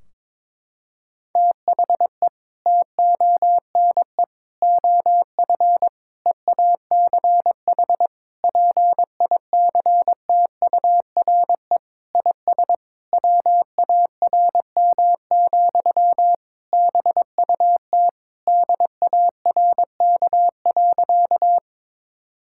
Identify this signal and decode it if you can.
Morse code, heard as a keyed tone, about 22 wpm, 700 Hz. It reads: THE TONE OF EACH PICTURE IS WARM, BUT DARK.
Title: Sentences from Top 1600 Words 22wpm